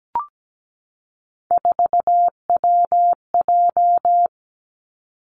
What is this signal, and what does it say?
Morse code, heard as a keyed tone, about 17 wpm, 700 Hz. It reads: 4WJ
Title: Three Challenging Letter-Number Combinations 17wpm